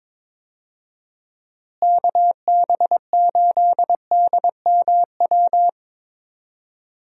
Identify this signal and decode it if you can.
Morse code, heard as a keyed tone, about 21 wpm, 700 Hz. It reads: KB8DMW
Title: USA Call Signs - 6 Characters - Repeated 2x - 22wpm